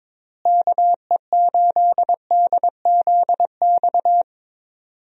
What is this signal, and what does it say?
Morse code, heard as a keyed tone, about 22 wpm, 700 Hz. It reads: KE8DZX
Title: USA Call Signs - 6 Characters - Repeated 3x - 22wpm